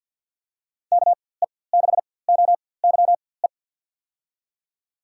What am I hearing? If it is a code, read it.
Morse code, heard as a keyed tone, about 50 wpm, spaced as for 23 wpm, 700 Hz. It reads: KE6OYE
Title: USA Call Signs - 6 Characters - 4x Character Spacing - 50wpm